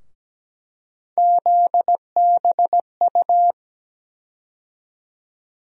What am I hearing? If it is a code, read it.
Morse code, heard as a keyed tone, about 17 wpm, 700 Hz. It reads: ZBU
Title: USA Call Sign Postfixes - 3 Letters 17wpm